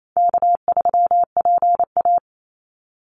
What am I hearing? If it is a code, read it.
Morse code, heard as a keyed tone, about 28 wpm, 700 Hz. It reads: K3PA
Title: Easy Call Signs - 28wpm